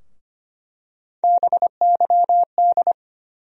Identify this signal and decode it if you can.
Morse code, heard as a keyed tone, about 25 wpm, 700 Hz. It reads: BYD
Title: USA Call Sign Postfixes - 3 Letters 25wpm